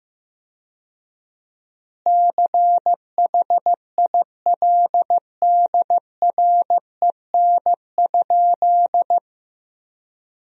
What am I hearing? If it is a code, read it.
Morse code, heard as a keyed tone, about 15 wpm, 700 Hz. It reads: CHILDREN?
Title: 4x Word-Spacing - Sentences from Top 300 Words - 15wpm